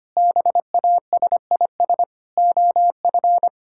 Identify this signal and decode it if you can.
Morse code, heard as a keyed tone, about 25 wpm, 700 Hz. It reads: BASIS OF
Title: Sentences from Top 1600 Words 25wpm